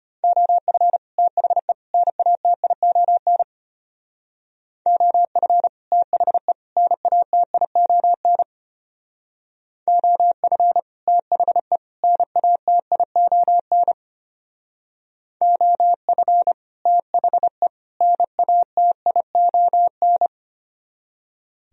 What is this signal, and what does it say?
Morse code, heard as a keyed tone, about 32 wpm, 700 Hz. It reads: OF THE NATION OF THE NATION OF THE NATION OF THE NATION